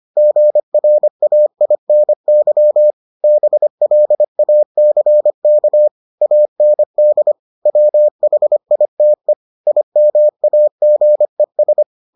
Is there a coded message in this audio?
Yes: Morse code from 0.2 to 11.8 s, heard as a keyed tone, about 25 wpm, 600 Hz. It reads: GRAINY BLACK AND WHITE IMAGES